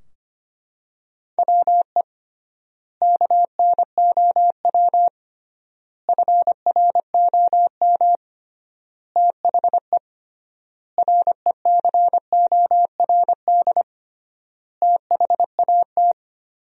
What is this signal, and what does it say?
Morse code, heard as a keyed tone, about 25 wpm, 700 Hz. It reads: WE KNOW FROM THE RECORD THAT
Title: 3x Word-Spacing - Sentences from Top 500 Words - 25wpm